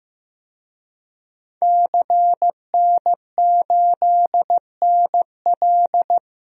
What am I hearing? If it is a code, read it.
Morse code, heard as a keyed tone, about 15 wpm, 700 Hz. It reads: CN8NL
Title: DXCC-100 Call Signs - 5 Characters - 15wpm